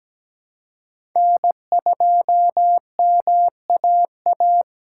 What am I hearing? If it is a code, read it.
Morse code, heard as a keyed tone, about 17 wpm, 700 Hz. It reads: N2MAA
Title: USA Call Signs - 4x Character Spacing - 17wpm